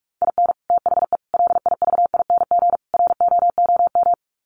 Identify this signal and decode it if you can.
Morse code, heard as a keyed tone, about 44 wpm, 700 Hz. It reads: IN THE LIVING ROOM
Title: Sets of 4 Words 45wpm